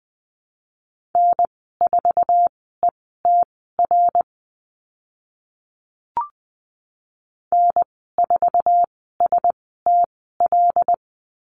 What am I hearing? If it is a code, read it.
Morse code, heard as a keyed tone, about 20 wpm, spaced as for 14 wpm, 700 Hz. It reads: N4ETR N4STL
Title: 082 Lesson 14g - Number 2 - Call Signs 20wpm